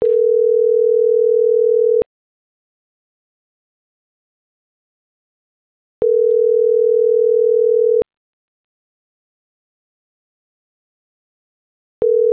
Ringback tone (North America), 440 + 480 Hz.
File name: CENTRAL SAGOT / out isai.hernandez - .79258